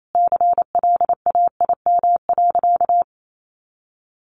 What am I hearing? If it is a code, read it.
Morse code, heard as a keyed tone, about 28 wpm, 700 Hz. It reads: CLAIM.
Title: Sentences from Top 1500 Words 28wpm